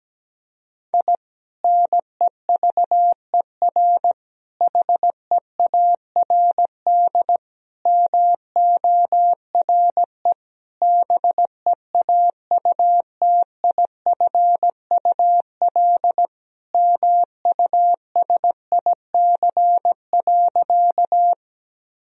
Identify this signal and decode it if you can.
Morse code, heard as a keyed tone, about 17 wpm, 700 Hz. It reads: I NEVER HEARD MORE BEAUTIFUL MUSIC.
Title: Sentences from Top 1000 Words - Review Part 1 - 17wpm